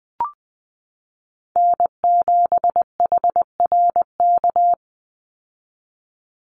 Morse code, heard as a keyed tone, about 20 wpm, 700 Hz. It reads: N7HRK
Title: Call Signs - 20wpm